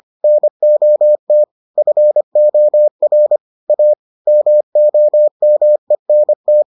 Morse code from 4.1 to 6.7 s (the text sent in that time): MOMENT